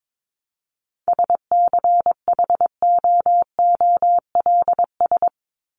Morse code, heard as a keyed tone, about 22 wpm, 700 Hz. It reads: SCHOOLS
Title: ES:S Suffix - Top 100 Words with Base Words 22wpm